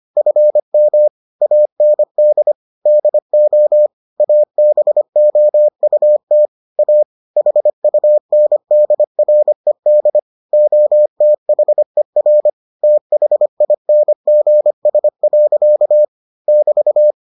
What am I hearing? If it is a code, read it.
Morse code, heard as a keyed tone, about 25 wpm, 600 Hz. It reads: FM AND DO ABOUT A HUNDRED OTHER THINGS. <BT>